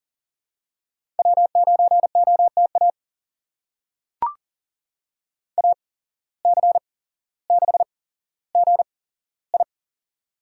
Morse code, heard as a keyed tone, about 38 wpm, spaced as for 17 wpm, 700 Hz. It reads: W9OTA AC6GI
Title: USA Call Signs - 5 Characters - 8x Character Spacing - 40wpm